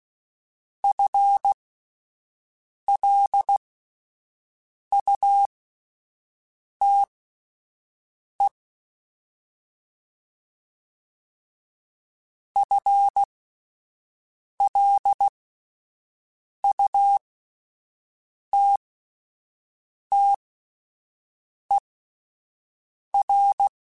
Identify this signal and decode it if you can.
Morse code, heard as a keyed tone, about 16 wpm, spaced as for 5 wpm, 800 Hz. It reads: FLUTE FLUTTER
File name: Spaced / FL Runs sp mp3s / FL Runs sp @10